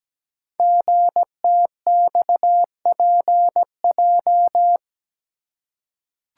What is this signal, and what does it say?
Morse code, heard as a keyed tone, about 17 wpm, 700 Hz. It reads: GTXPJ
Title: Code Groups - 5 Letters - 17wpm